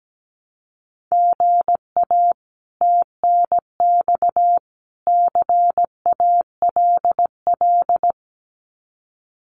Morse code, heard as a keyed tone, about 17 wpm, 700 Hz. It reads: GA TNX CALL
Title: Common QSO Phrases 17wpm